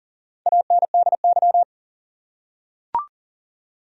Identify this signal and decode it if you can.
Morse code, heard as a keyed tone, about 40 wpm, 700 Hz. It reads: ANDY
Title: USA Common Names - Max 4 Letters - 40wpm